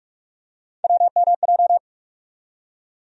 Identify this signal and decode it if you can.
Morse code, heard as a keyed tone, about 45 wpm, 700 Hz. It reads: WMJ
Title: USA Call Sign Postfixes - 3 Letters 45wpm